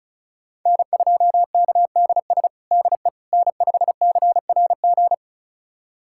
Morse code, heard as a keyed tone, about 36 wpm, 700 Hz. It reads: N2KDS DE N5CRG
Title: USA Callsign DE Callsign - 5 Characters - 35wpm